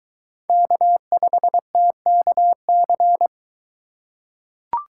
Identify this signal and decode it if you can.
Morse code, heard as a keyed tone, about 22 wpm, 700 Hz. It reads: K5TKC